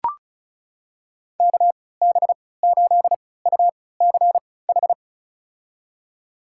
Morse code, heard as a keyed tone, about 34 wpm, spaced as for 20 wpm, 700 Hz. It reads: KB8UCH
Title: USA Call Signs - 6 Characters - 3x Character Spacing - 35wpm